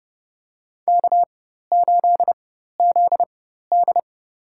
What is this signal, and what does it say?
Morse code, heard as a keyed tone, about 29 wpm, spaced as for 14 wpm, 700 Hz. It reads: K8ZD
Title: USA Call Signs - 4 Characters - 4x Character Spacing - 30wpm